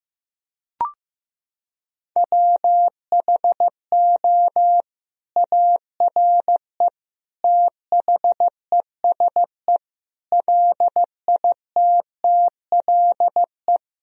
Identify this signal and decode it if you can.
Morse code, heard as a keyed tone, about 15 wpm, 700 Hz. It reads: WHO ARE THESE LITTLE
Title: Sentences from Top 1000 Words - Review Part 1 - 15wpm